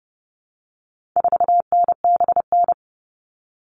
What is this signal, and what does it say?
Morse code, heard as a keyed tone, about 30 wpm, 700 Hz. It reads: 4NBN
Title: All Cut Numbers - 4-Digit 30wpm